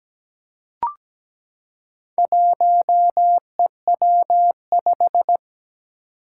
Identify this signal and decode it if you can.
Morse code, heard as a keyed tone, about 17 wpm, 700 Hz. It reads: 1EW5
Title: Four Challenging Letter-Number Combinations 17wpm